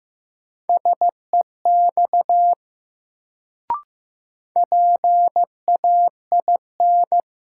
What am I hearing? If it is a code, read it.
Morse code, heard as a keyed tone, about 15 wpm, 700 Hz. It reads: SEX PAIN
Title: Sentences from Top 1600 Words - New Word Drill Alt - 15wpm